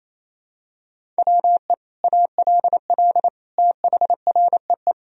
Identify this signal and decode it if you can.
Morse code, heard as a keyed tone, about 28 wpm, 700 Hz. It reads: WE ALL THREE
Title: Sentences from Top 1000 Words - Review Part 1 - 28wpm